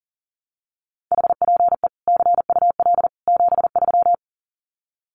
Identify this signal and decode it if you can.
Morse code, heard as a keyed tone, about 40 wpm, 700 Hz. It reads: HPE CUL 73
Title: Common QSO Phrases 40wpm